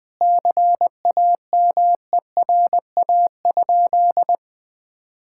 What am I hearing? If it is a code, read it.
Morse code, heard as a keyed tone, about 20 wpm, 700 Hz. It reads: CAMERA?
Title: Sentences from Top 1000 Words - Review Part 5 - 20wpm